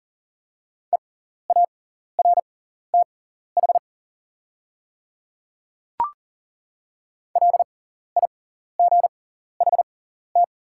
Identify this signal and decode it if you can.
Morse code, heard as a keyed tone, about 40 wpm, spaced as for 14 wpm, 700 Hz. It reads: EARTH LIGHT
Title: 6x Character Spacing - Top 200 Words - 40wpm